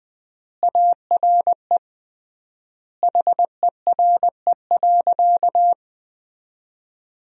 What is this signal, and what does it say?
Morse code, heard as a keyed tone, about 21 wpm, 700 Hz. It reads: ARE HERE.
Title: 3x Word-Spacing - Sentences from Top 200 Words - 20wpm